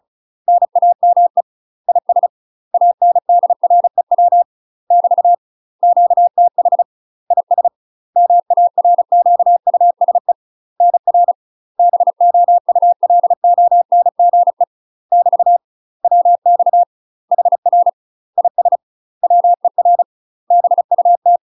Morse code, heard as a keyed tone, about 34 wpm, 700 Hz. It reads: NAME IS ANDREW <BT> QTH IS MARQUSE NR BOULONGE <BT> WX HR IS WER BUT
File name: UK MASTER QSOs TRAINING / 35WPM / UK MASTER QSOs 0008